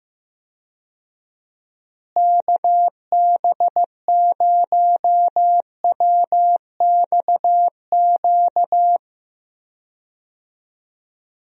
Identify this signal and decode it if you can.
Morse code, heard as a keyed tone, about 15 wpm, 700 Hz. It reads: KB0WXQ
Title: USA Call Signs - 6 Characters - Repeated 2x - 15wpm